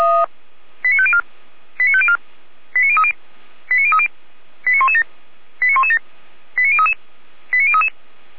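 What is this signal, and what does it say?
Five-tone selective call (ZVEI), heard as quick runs of five tones, tones of about 70 ms. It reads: 89504 89504 89030 89030 89117 89117 89033 89033
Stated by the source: ZVEI five-tone selective call received on an SDR.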